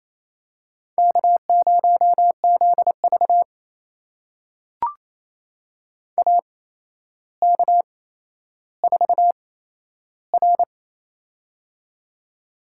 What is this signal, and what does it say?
Morse code, heard as a keyed tone, about 28 wpm, spaced as for 12 wpm, 700 Hz. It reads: K0ZV AK4R
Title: USA Call Signs - 4 Characters - 8x Character Spacing - 28wpm